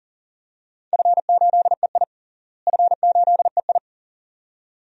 Morse code, heard as a keyed tone, about 40 wpm, 700 Hz. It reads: F8EI F8EI
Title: DXCC-100 Call Signs - 4 Characters - Repeated 2x - 40wpm